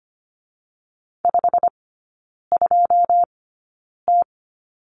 Morse code, heard as a keyed tone, about 25 wpm, spaced as for 9 wpm, 700 Hz. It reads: 52T
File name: Lesson 17e - Letter u - Three Characters 25wpm